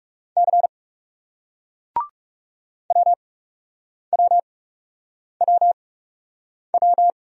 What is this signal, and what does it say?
Morse code, heard as a keyed tone, about 40 wpm, spaced as for 8 wpm, 700 Hz. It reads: C WWWW